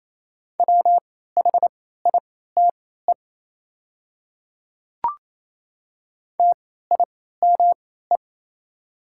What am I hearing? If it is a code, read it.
Morse code, heard as a keyed tone, about 29 wpm, spaced as for 16 wpm, 700 Hz. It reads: WHITE TIME
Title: 3x Character Spacing - Top 300 Words - 28wpm